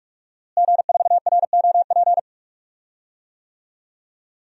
Morse code, heard as a keyed tone, about 46 wpm, 700 Hz. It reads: G4ROP
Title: DXCC-100 Call Signs - 5 Characters - Repeated 2x - 45wpm